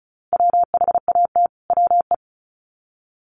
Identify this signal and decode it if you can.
Morse code, heard as a keyed tone, about 34 wpm, 700 Hz. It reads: WHAT WE
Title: Sets of 2 Words 35wpm